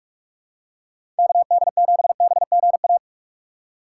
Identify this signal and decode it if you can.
Morse code, heard as a keyed tone, about 44 wpm, 700 Hz. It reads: KD7BGA
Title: USA Call Signs - 6 Characters - 6x Character Spacing - 45wpm